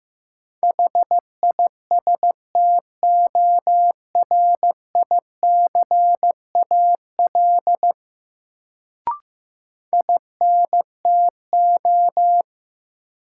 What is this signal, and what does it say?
Morse code, heard as a keyed tone, about 16 wpm, 700 Hz. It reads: HISTORICAL INTO
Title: Sentences from Top 600 Words - New Word Drill - 15wpm